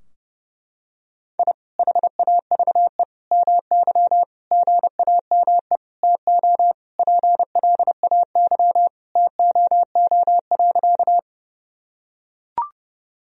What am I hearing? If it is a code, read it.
Morse code, heard as a keyed tone, about 30 wpm, 700 Hz. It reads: I HAVE MY GAME TO PLAY TOO.
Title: Sentences from Top 1000 Words - Review Part 3 - 30wpm